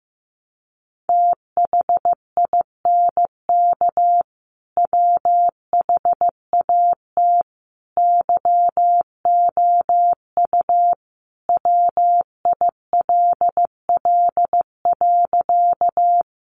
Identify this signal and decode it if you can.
Morse code, heard as a keyed tone, about 15 wpm, 700 Hz. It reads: THINK WHAT YOU WILL.
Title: Sentences from Top 200 Words 15wpm